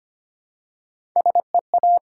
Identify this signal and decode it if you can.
Morse code, heard as a keyed tone, about 25 wpm, 700 Hz. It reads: SEA
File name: practice / 25wpm / Difficult Words with Hints - 25wpm